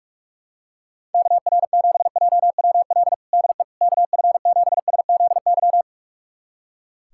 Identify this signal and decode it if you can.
Morse code, heard as a keyed tone, about 43 wpm, 700 Hz. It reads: KR7JWL DE KF7SZY
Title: USA Callsign DE Callsign - 6 Characters - 45wpm